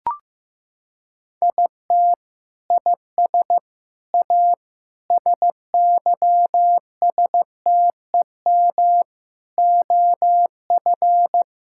Morse code, heard as a keyed tone, about 15 wpm, 700 Hz. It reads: IT IS A SYSTEM OF
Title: Sentences from Top 1000 Words - Review Part 6 - 15wpm